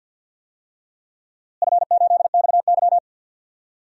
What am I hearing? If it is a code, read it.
Morse code, heard as a keyed tone, about 49 wpm, 700 Hz. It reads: F8XY
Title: DXCC-100 Call Signs - 4 Characters - Repeated 2x - 50wpm